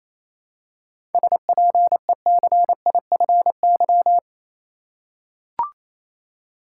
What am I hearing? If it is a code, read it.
Morse code, heard as a keyed tone, about 28 wpm, 700 Hz. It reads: SPECIFY
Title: Seven Letter Words - 28wpm